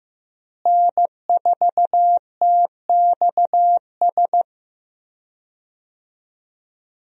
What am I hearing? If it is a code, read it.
Morse code, heard as a keyed tone, about 15 wpm, 700 Hz. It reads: N4TXS
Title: USA Call Signs - 5 Characters - Repeated 3x - 15wpm